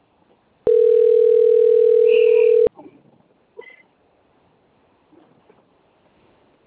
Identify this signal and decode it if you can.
Ringback tone (North America), 440 + 480 Hz.